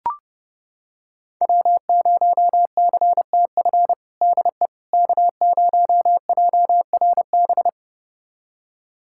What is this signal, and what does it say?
Morse code, heard as a keyed tone, about 32 wpm, 700 Hz. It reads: W0CTF DE K0JRB
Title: USA Callsign DE Callsign - 5 Characters - 30wpm